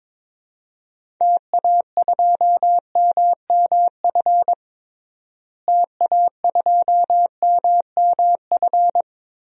Morse code, heard as a keyed tone, about 21 wpm, 700 Hz. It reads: TA2MMF TA2MMF
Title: DXCC-100 Call Signs - 6 Characters - Repeated 3x - 22wpm